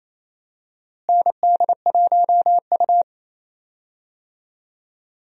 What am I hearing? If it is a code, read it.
Morse code, heard as a keyed tone, about 28 wpm, 700 Hz. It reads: ND1U
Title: USA Call Signs - 4 Characters - Repeated 2x - 28wpm